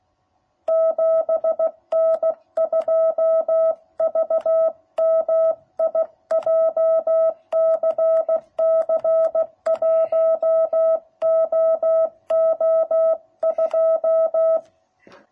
Morse code, heard as a keyed tone, about 16 wpm, 660 Hz. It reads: 7N2VMI JCC 1OO2